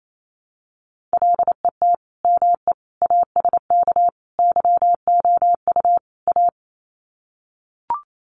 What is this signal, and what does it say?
Morse code, heard as a keyed tone, about 28 wpm, 700 Hz. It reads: LET ME ASK YOU A
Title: Sets of 5 Words Encore 28wpm